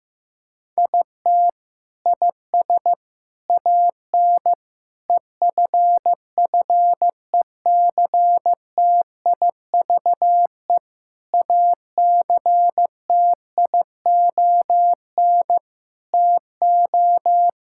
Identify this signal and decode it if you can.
Morse code, heard as a keyed tone, about 15 wpm, 700 Hz. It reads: IT IS AN EFFECTIVE ACTION TO
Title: Sentences from Top 1000 Words - Review Part 6 - 15wpm